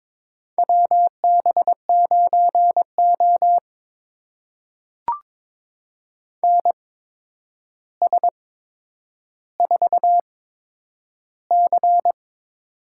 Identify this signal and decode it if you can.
Morse code, heard as a keyed tone, about 22 wpm, spaced as for 9 wpm, 700 Hz. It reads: WB9O NS4C
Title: USA Call Signs - 4 Characters - 8x Character Spacing - 22wpm